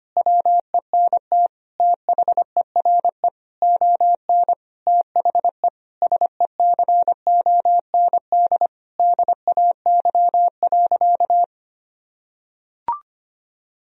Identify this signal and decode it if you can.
Morse code, heard as a keyed tone, about 25 wpm, 700 Hz. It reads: WENT THERE ON THE SECOND DAY.
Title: Sentences from Top 1000 Words - Review Part 4 - 25wpm